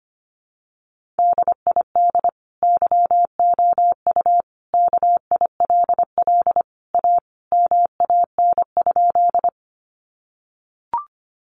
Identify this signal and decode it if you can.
Morse code, heard as a keyed tone, about 25 wpm, 700 Hz. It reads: DID YOU KILL A MAN?
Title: Sentences from Top 1400 Words 25wpm